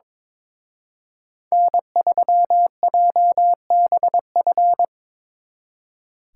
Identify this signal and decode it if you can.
Morse code, heard as a keyed tone, about 24 wpm, 700 Hz. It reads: N3JBF
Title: Speed Racing - USA 1x3 Call Signs 22wpm